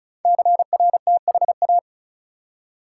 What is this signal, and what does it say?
Morse code, heard as a keyed tone, about 36 wpm, 700 Hz. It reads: CRTHA
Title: Code Groups - 5 Letters - 35wpm